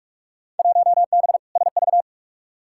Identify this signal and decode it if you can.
Morse code, heard as a keyed tone, about 46 wpm, 700 Hz. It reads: 1B SV
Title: Field Day Exchanges - 45wpm